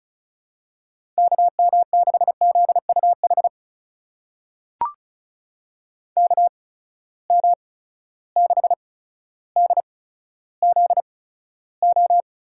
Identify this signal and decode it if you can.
Morse code, heard as a keyed tone, about 36 wpm, spaced as for 15 wpm, 700 Hz. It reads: KM6ZUH KM6DZO
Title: USA Call Signs - 6 Characters - 8x Character Spacing - 35wpm